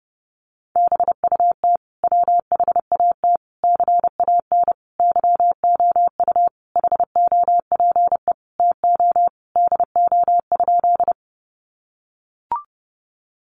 Morse code, heard as a keyed tone, about 30 wpm, 700 Hz. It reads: BUT WHAT CAN YOU HOPE TO DO?
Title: Sentences from Top 1000 Words 30wpm